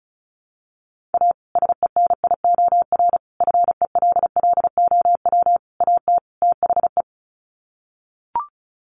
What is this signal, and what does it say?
Morse code, heard as a keyed tone, about 34 wpm, 700 Hz. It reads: A SENIOR FELLOW AT THE
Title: Sets of 5 Words Encore 35wpm